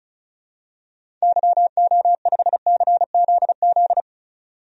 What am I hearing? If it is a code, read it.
Morse code, heard as a keyed tone, about 36 wpm, 700 Hz. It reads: YO5CZZ
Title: DXCC-100 Call Signs - 6 Characters - 35wpm